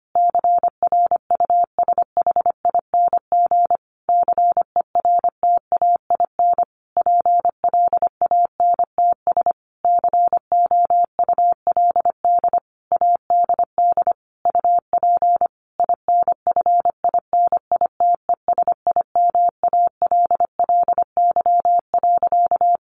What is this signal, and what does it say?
Morse code, heard as a keyed tone, about 25 wpm, 700 Hz. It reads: CRUSHING CERTAIN PLANTS COULD ADD UP INFINITESIMALLY.
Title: A Sound Of Thunder 25wpm